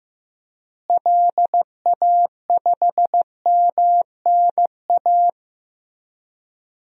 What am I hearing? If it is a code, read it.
Morse code, heard as a keyed tone, about 15 wpm, 700 Hz. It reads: LA5MNA
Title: DXCC-100 Call Signs - 6 Characters - Repeated 2x - 15wpm